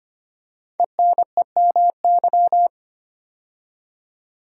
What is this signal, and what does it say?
Morse code, heard as a keyed tone, about 25 wpm, 700 Hz. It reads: ENEMY
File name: 5 Letter Words - First 3 Letters Given - 25wpm